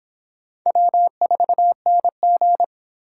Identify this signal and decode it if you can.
Morse code, heard as a keyed tone, about 26 wpm, 700 Hz. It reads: W4NG
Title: Speed Racing - USA 1x2 Call Signs 22wpm